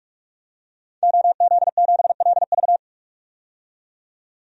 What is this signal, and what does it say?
Morse code, heard as a keyed tone, about 46 wpm, 700 Hz. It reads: OZ7LV
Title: DXCC-100 Call Signs - 5 Characters - 45wpm